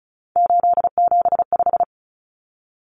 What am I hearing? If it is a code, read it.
Morse code, heard as a keyed tone, about 34 wpm, 700 Hz. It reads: 875
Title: Numbers - 3-Digit 35wpm